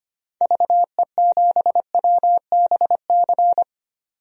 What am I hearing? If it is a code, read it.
Morse code, heard as a keyed tone, about 25 wpm, 700 Hz. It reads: VE7WBC
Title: DXCC-100 Call Signs - 6 Characters - 25wpm